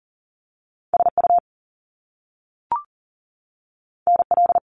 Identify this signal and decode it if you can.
Morse code, heard as a keyed tone, about 40 wpm, 700 Hz. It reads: SU NL
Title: Top Two Letter Combinations in Words 40wpm